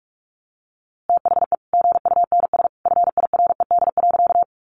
Morse code, heard as a keyed tone, about 44 wpm, 700 Hz. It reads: THE GUNS FIRED.
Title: A Sound Of Thunder 45wpm